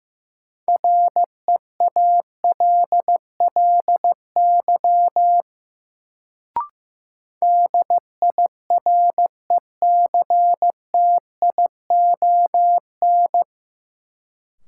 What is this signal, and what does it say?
Morse code, heard as a keyed tone, about 15 wpm, 700 Hz. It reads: REALLY DIRECTION
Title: Sets of 4 Words - New Word Drill - 15wpm